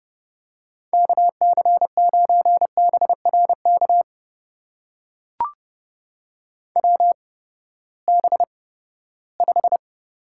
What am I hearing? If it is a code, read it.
Morse code, heard as a keyed tone, about 31 wpm, spaced as for 17 wpm, 700 Hz. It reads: KC9BRK WB5
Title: USA Call Signs - 6 Characters - 8x Character Spacing - 30wpm